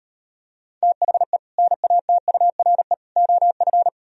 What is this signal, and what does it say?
Morse code, heard as a keyed tone, about 38 wpm, 700 Hz. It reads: THE NATURE OF